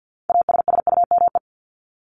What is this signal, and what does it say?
Morse code, heard as a keyed tone, about 50 wpm, 700 Hz. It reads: ASSUME
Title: Top 1251-1500 Words 50wpm